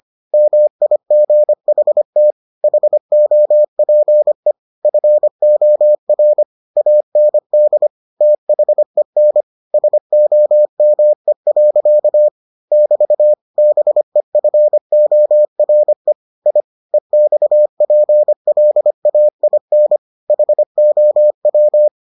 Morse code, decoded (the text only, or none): MIGHT HOPE FOR AND THEN SOME. <BT> BEFORE I EXPLAIN HOW